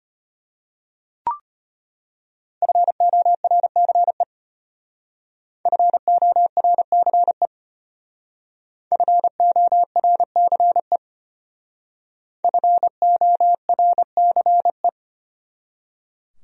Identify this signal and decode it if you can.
Morse code, heard as a keyed tone, about 32 wpm, 700 Hz. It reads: FORCE FORCE FORCE FORCE